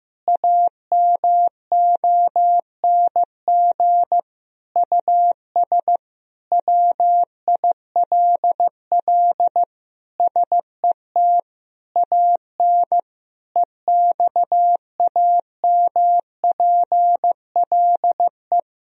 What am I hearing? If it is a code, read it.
Morse code, heard as a keyed tone, about 15 wpm, 700 Hz. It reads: AMONG US WILL SET AN EXAMPLE
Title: Sentences from Top 1000 Words - Review Part 3 - 15wpm